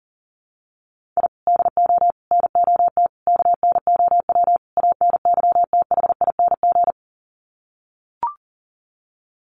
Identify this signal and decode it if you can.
Morse code, heard as a keyed tone, about 40 wpm, 700 Hz. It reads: I DO NOT KNOW ANYTHING
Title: Sets of 5 Words Encore 40wpm